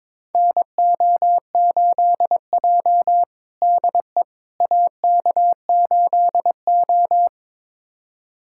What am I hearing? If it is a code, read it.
Morse code, heard as a keyed tone, about 22 wpm, 700 Hz. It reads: NO8J DE AK8O
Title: USA Callsign DE Callsign - 4 Characters - 22wpm